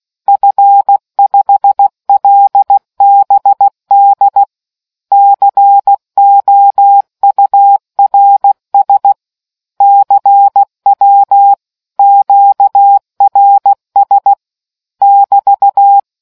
Morse code, heard as a keyed tone, about 16 wpm, 800 Hz. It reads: F5LBD COURS CW QRS <BT>